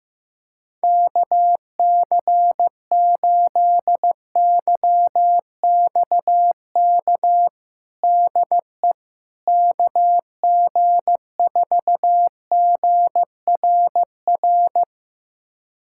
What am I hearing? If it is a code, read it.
Morse code, heard as a keyed tone, about 16 wpm, 700 Hz. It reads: KC8YXK DE KG4GRR
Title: USA Callsign DE Callsign - 6 Characters - 15wpm